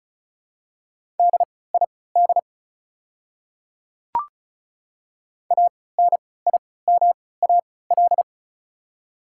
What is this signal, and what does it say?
Morse code, heard as a keyed tone, about 36 wpm, spaced as for 20 wpm, 700 Hz. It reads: DID ANIMAL
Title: 3x Character Spacing - Top 300 Words - 35wpm